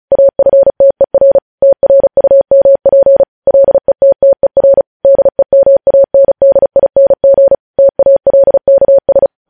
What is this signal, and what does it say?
Morse code, heard as a keyed tone, about 36 wpm, 550 Hz. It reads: AFTER TRUMP LETTER DEMANDING TALKS
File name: podcast / News headlines-1x-35x35